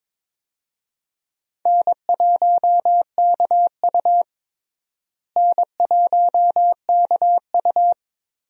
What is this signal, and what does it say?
Morse code, heard as a keyed tone, about 22 wpm, 700 Hz. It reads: N1KU N1KU
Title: USA Call Signs - 4 Characters - Repeated 2x - 22wpm